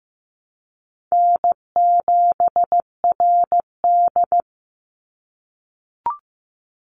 Morse code, heard as a keyed tone, about 15 wpm, 700 Hz. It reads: N7RD
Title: Easy Call Signs - 15wpm